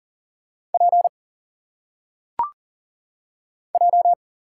Morse code, heard as a keyed tone, about 40 wpm, 700 Hz. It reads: P J
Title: Single Letters - Phonetic Pronunciation 40wpm